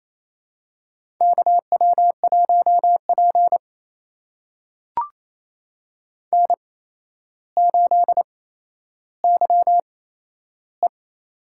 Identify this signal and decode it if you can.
Morse code, heard as a keyed tone, about 28 wpm, spaced as for 12 wpm, 700 Hz. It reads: KW1P N8YE
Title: USA Call Signs - 4 Characters - 8x Character Spacing - 28wpm